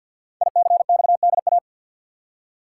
Morse code, heard as a keyed tone, about 53 wpm, 700 Hz. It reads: ICXDA
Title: Code Groups - 5 Letter:Numbers - 50wpm